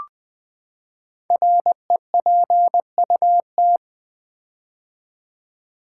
Morse code, heard as a keyed tone, about 20 wpm, 700 Hz. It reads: REPUT
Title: Build A Word - 5 Letters At A Time - 10-Letter Words - 20wpm